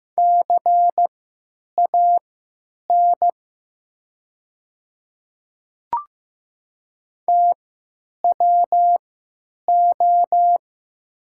Morse code, heard as a keyed tone, about 15 wpm, spaced as for 9 wpm, 700 Hz. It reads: CAN TWO
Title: 3x Character Spacing - Top 100 Words - 15wpm